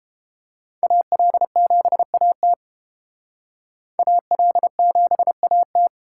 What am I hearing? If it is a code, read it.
Morse code, heard as a keyed tone, about 29 wpm, 700 Hz. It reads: AL7AT AL7AT